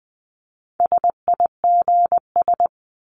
Morse code, heard as a keyed tone, about 20 wpm, 700 Hz. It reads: SIGS
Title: Abbreviations 20wpm